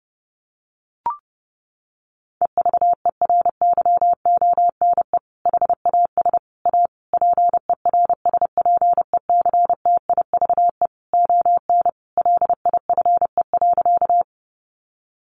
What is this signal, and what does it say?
Morse code, heard as a keyed tone, about 30 wpm, 700 Hz. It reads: EVERYONE HAS A PERSPECTIVE ON LIFE.
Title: Sentences from Top 1200 Words 30wpm